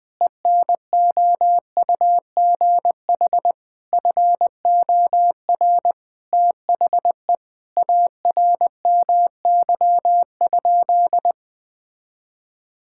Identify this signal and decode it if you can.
Morse code, heard as a keyed tone, about 21 wpm, 700 Hz. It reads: ENOUGH FOR THE ARMY?
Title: Sentences from Top 1000 Words - Review Part 5 - 20wpm